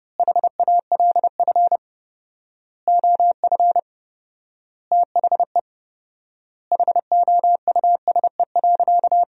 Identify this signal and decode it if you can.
Morse code, heard as a keyed tone, about 30 wpm, 700 Hz. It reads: HALF OF THE HOUSE.
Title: 4x Word-Spacing - Sentences from Top 400 Words - 30wpm